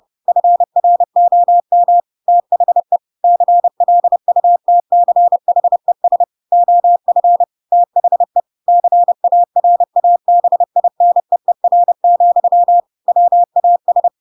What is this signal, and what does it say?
Morse code, heard as a keyed tone, about 30 wpm, 700 Hz. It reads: FROM THE CLUTCHES OF THE CARABINEER, WAS